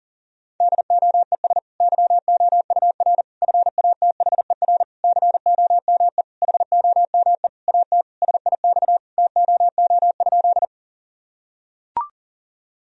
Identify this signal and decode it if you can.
Morse code, heard as a keyed tone, about 40 wpm, 700 Hz. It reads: DOES YOUR FATHER COME HOME AT SIX TOO?
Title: Sentences from Top 1000 Words - Review Part 2 - 40wpm